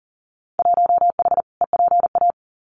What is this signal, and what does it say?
Morse code, heard as a keyed tone, about 41 wpm, 700 Hz. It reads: 1H EPA
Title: Winter Field Day Exchanges 40wpm